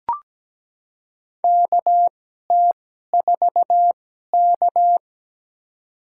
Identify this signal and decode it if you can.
Morse code, heard as a keyed tone, about 17 wpm, spaced as for 12 wpm, 700 Hz. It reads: KT4K
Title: USA Call Signs - 4 Characters - 2x Character Spacing - 17wpm